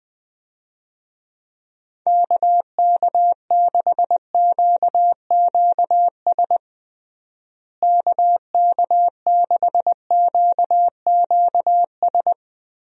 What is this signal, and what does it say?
Morse code, heard as a keyed tone, about 21 wpm, 700 Hz. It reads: KK6QQS KK6QQS
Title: USA Call Signs - 6 Characters - Repeated 2x - 20wpm